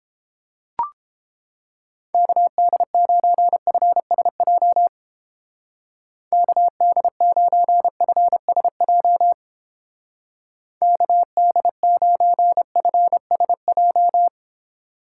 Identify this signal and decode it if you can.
Morse code, heard as a keyed tone, about 29 wpm, 700 Hz. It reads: KD9FSJ KD9FSJ KD9FSJ